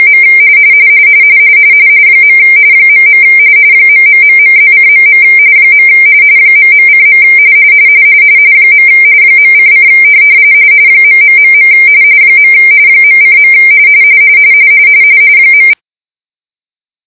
RTTY, about 45.45 baud. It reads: M . .. ....+:.. ..:. .:: .... .:. .=